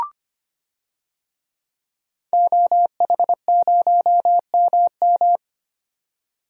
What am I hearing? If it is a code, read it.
Morse code, heard as a keyed tone, about 25 wpm, 700 Hz. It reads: OH0MM
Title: DXCC-100 Call Signs - 5 Characters - 25wpm